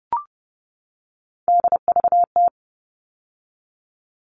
Morse code, heard as a keyed tone, about 30 wpm, 700 Hz. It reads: DVT
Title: All Cut Numbers - 3-Digit 30wpm